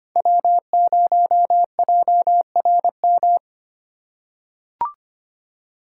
Morse code, heard as a keyed tone, about 25 wpm, 700 Hz. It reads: W0JRM
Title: USA Call Signs - 5 Characters - 3x Character Spacing - 25wpm